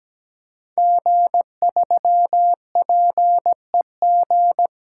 Morse code, heard as a keyed tone, about 17 wpm, 700 Hz. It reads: G3PEG
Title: Code Groups - 5 Letter:Numbers - 17wpm